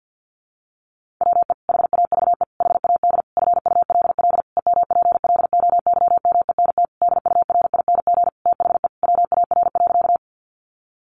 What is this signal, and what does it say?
Morse code, heard as a keyed tone, about 48 wpm, 700 Hz. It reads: WE HAVE HAD FULL EMPLOYMENT DURING THE WAR.
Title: Sentences from Top 1500 Words 50wpm